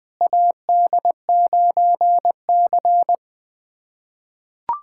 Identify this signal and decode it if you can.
Morse code, heard as a keyed tone, about 20 wpm, 700 Hz. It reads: AD9C
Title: USA Callsign DE Callsign - 4 Characters - 20wpm